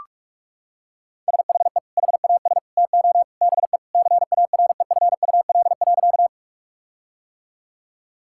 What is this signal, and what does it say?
Morse code, heard as a keyed tone, about 46 wpm, 700 Hz. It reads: SHE HAS TO BE CAREFUL.
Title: Sentences from Top 1600 Words 45wpm